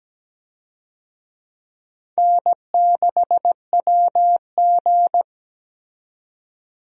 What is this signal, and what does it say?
Morse code, heard as a keyed tone, about 17 wpm, 700 Hz. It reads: N6WG
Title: USA Call Signs - 4 Characters - 17wpm